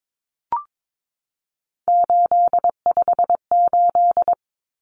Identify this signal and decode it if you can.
Morse code, heard as a keyed tone, about 22 wpm, 700 Hz. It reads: 858
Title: Numbers - 3-Digit 22wpm